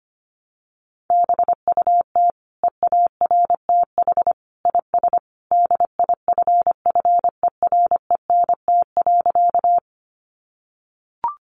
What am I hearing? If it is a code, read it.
Morse code, heard as a keyed tone, about 25 wpm, 700 Hz. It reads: BUT EARTH IS DIFFERENT.
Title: Sentences from Top 600 Words 25wpm